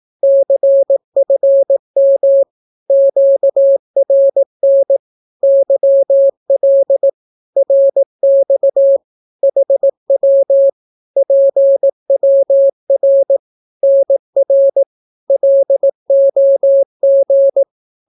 Morse code, heard as a keyed tone, about 18 wpm, 550 Hz. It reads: CFM QRN YL RX HW PWR NR LOG
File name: naukapodstaw / skroty / skroty